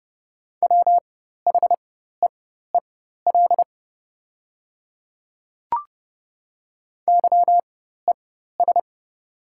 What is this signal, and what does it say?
Morse code, heard as a keyed tone, about 30 wpm, spaced as for 14 wpm, 700 Hz. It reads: WHEEL YES
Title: 4x Character Spacing - Top 500 Words - 30wpm